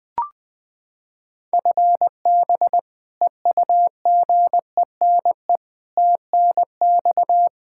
Morse code, heard as a keyed tone, about 20 wpm, 700 Hz. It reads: FB EUGENE TNX
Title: Practice QSO - Exchange 3 - More Info - 20wpm